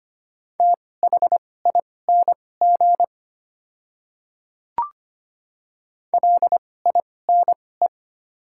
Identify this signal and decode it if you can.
Morse code, heard as a keyed tone, about 25 wpm, spaced as for 18 wpm, 700 Hz. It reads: THING LINE
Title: 2x Character Spacing - Top 200 Words - 25wpm